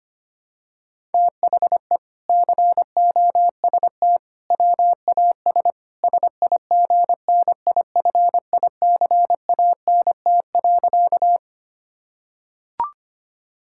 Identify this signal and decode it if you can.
Morse code, heard as a keyed tone, about 25 wpm, 700 Hz. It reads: THE COST WAS SIGNIFICANT.
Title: Sentences from Top 1000 Words - Review Part 6 - 25wpm